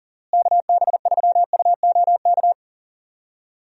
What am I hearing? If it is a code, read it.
Morse code, heard as a keyed tone, about 39 wpm, 700 Hz. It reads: KB3UOK
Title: USA Call Signs - 6 Characters - Repeated 3x - 40wpm